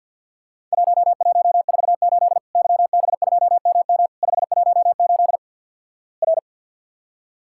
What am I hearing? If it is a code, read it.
Morse code, heard as a keyed tone, about 48 wpm, 700 Hz. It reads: 1148 YB2MM 518 R
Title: Worked All Europe DX Contest - QTC Exchange - 50wpm